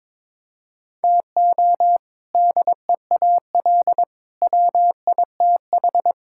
Morse code, heard as a keyed tone, about 22 wpm, 700 Hz. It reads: TO DEAL WITH
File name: practice / 22wpm / Sentences with 3-Word Phrases - 22wpm